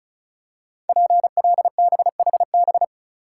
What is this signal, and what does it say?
Morse code, heard as a keyed tone, about 35 wpm, 700 Hz. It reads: PLBHB
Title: Code Groups - 5 Letter:Numbers - 35wpm